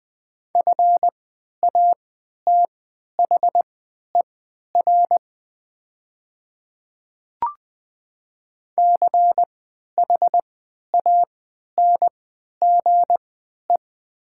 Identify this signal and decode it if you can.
Morse code, heard as a keyed tone, about 20 wpm, spaced as for 11 wpm, 700 Hz. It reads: FATHER CHANGE
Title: 3x Character Spacing - Top 200 Words - 20wpm